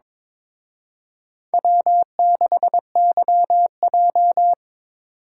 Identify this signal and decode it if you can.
Morse code, heard as a keyed tone, about 25 wpm, 700 Hz. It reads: W6YJ
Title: Speed Racing - USA 1x2 Call Signs 22wpm